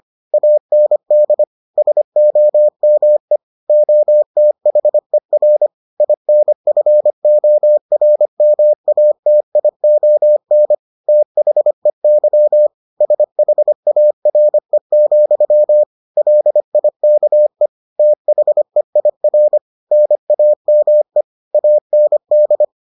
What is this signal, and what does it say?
Morse code, heard as a keyed tone, about 25 wpm, 600 Hz. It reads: AND SOME OTHER INFORMATION THEY SHARE, LIKE THEIR NAME AND